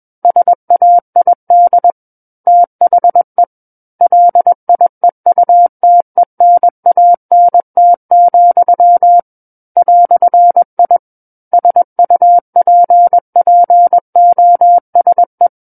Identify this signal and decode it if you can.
Morse code, heard as a keyed tone, about 21 wpm, 700 Hz. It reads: SAID THE LIEUTENANT, "I SUPPOSE
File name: Book One / 12-Chapter 12 - 21 WPM